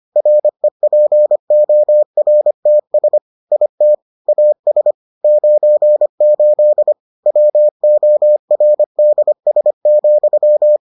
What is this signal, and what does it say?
Morse code, heard as a keyed tone, about 25 wpm, 600 Hz. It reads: REPORTS IT AS 98 WORDS,